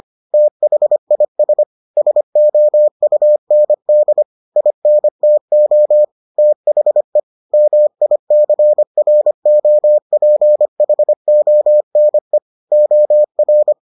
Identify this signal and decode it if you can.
Morse code, heard as a keyed tone, about 25 wpm, 600 Hz. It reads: THIS SOUND INTO THE MICROPHONE OR